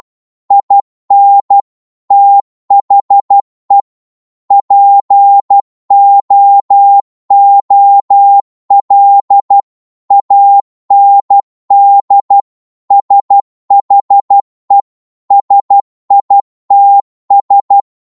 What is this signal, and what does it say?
Morse code, heard as a keyed tone, about 12 wpm, 800 Hz. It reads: IN THE POOL AND SHE SITS
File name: Morse / psg-12wpm